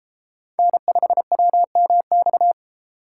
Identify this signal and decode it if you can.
Morse code, heard as a keyed tone, about 32 wpm, 700 Hz. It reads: N5WMX